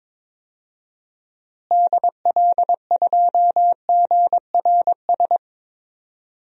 Morse code, heard as a keyed tone, about 21 wpm, 700 Hz. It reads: DL2GRS
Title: DXCC-100 Call Signs - 6 Characters - Repeated 3x - 22wpm